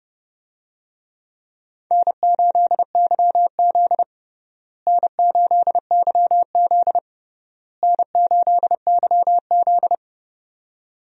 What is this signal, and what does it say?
Morse code, heard as a keyed tone, about 30 wpm, 700 Hz. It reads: N8YZ N8YZ N8YZ
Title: USA Call Signs - 4 Characters - Repeated 3x - 30wpm